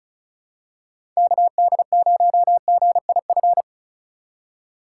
Morse code, heard as a keyed tone, about 36 wpm, 700 Hz. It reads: KD0GIF
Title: USA Call Signs - 6 Characters - 3x Character Spacing - 35wpm